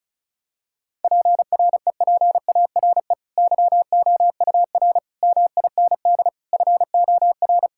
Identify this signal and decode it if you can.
Morse code, heard as a keyed tone, about 36 wpm, 700 Hz. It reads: PREPARE YOUR MIND FOR
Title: Sentences from Top 1000 Words - Review Part 6 - 35wpm